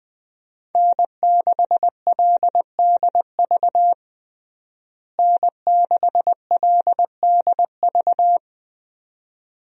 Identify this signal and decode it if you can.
Morse code, heard as a keyed tone, about 21 wpm, 700 Hz. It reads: N6LDV N6LDV
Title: USA Call Signs - 5 Characters - Repeated 2x - 20wpm